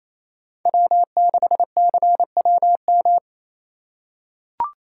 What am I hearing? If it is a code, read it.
Morse code, heard as a keyed tone, about 28 wpm, 700 Hz. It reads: W6CWM
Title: K1USN SST QSO Practice - 28wpm